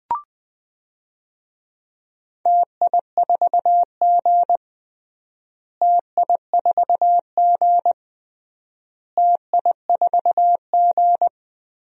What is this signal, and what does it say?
Morse code, heard as a keyed tone, about 21 wpm, 700 Hz. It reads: TI4G TI4G TI4G
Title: DXCC-100 Call Signs - 4 Characters - Repeated 3x - 20wpm